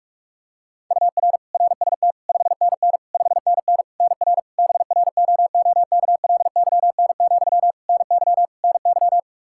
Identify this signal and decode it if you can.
Morse code, heard as a keyed tone, about 41 wpm, 700 Hz. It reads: UR RST 5NN 5NN NR BROOKLYN, NY NY
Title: K1USN SST QSO Practice 45wpm